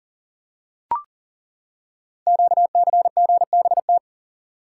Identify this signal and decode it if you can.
Morse code, heard as a keyed tone, about 39 wpm, 700 Hz. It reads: QCGBT
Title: Code Groups - 5 Letter:Numbers - 40wpm